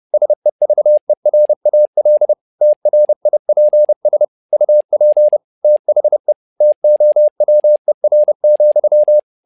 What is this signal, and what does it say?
Morse code, heard as a keyed tone, about 30 wpm, 600 Hz. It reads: SEVERAL TRIPS UP THE TOWER,
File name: learning center / tower trouble / tower trouble 30wpm